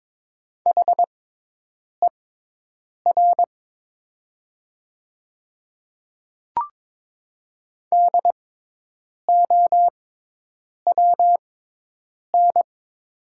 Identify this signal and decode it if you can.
Morse code, heard as a keyed tone, about 22 wpm, spaced as for 8 wpm, 700 Hz. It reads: HER DOWN
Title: 6x Character Spacing - Top 100 Words - 22wpm